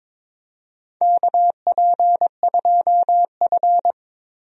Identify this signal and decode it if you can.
Morse code, heard as a keyed tone, about 22 wpm, 700 Hz. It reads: KP2F
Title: DXCC-100 Call Signs - 4 Characters - Repeated 3x - 22wpm